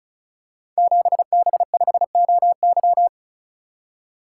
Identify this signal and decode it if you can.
Morse code, heard as a keyed tone, about 36 wpm, 700 Hz. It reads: 7B5OY
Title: Code Groups - 5 Letter:Numbers - 35wpm